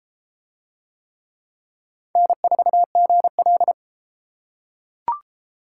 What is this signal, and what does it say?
Morse code, heard as a keyed tone, about 33 wpm, 700 Hz. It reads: N4GL